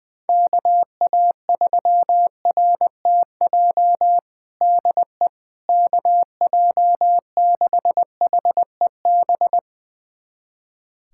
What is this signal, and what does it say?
Morse code, heard as a keyed tone, about 20 wpm, 700 Hz. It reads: KA3RTJ DE KJ6HEB
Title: USA Callsign DE Callsign 20wpm